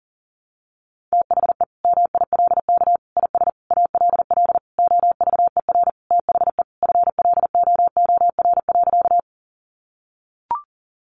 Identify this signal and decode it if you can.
Morse code, heard as a keyed tone, about 40 wpm, 700 Hz. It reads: THE MILK IS ALL OVER THE FLOOR.
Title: Additional Easy Phrases 40wpm